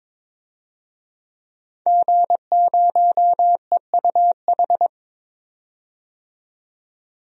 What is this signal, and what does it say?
Morse code, heard as a keyed tone, about 22 wpm, 700 Hz. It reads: G0EUH
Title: DXCC-100 Call Signs - 5 Characters - 22wpm